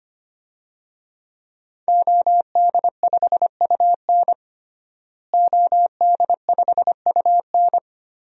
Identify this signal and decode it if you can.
Morse code, heard as a keyed tone, about 25 wpm, 700 Hz. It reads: OD5UN OD5UN
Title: DXCC-100 Call Signs - 5 Characters - Repeated 3x - 25wpm